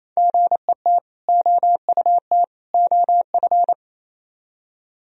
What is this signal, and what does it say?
Morse code, heard as a keyed tone, about 38 wpm, 700 Hz. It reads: GET OUT OF